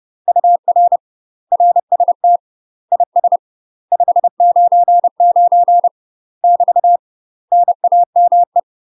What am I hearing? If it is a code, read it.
Morse code, heard as a keyed tone, about 30 wpm, 700 Hz. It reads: UR RST IS 599 <BT> NAME
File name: UK MASTER QSOs TRAINING / 30WPM / UK MASTER QSOs 0011